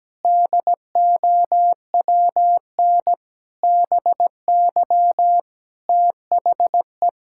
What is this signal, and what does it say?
Morse code, heard as a keyed tone, about 17 wpm, 700 Hz. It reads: DOWN BY THE
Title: Sentences from Top 1000 Words - Review Part 5 - 17wpm